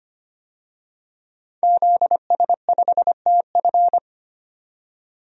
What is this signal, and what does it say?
Morse code, heard as a keyed tone, about 25 wpm, 700 Hz. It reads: ZS5TF
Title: DXCC-100 Call Signs - 5 Characters - 25wpm